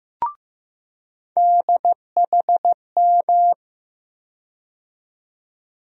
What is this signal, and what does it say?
Morse code, heard as a keyed tone, about 15 wpm, 700 Hz. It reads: DHM
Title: USA Call Sign Postfixes - 3 Letters 15wpm